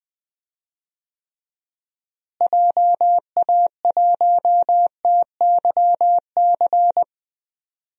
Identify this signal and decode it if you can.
Morse code, heard as a keyed tone, about 21 wpm, 700 Hz. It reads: JA1TYC
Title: DXCC-100 Call Signs - 6 Characters - Repeated 2x - 20wpm